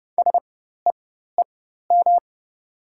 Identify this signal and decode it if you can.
Morse code, heard as a keyed tone, about 30 wpm, spaced as for 14 wpm, 700 Hz. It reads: SEEM
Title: 4x Character Spacing - Top 300 Words - 30wpm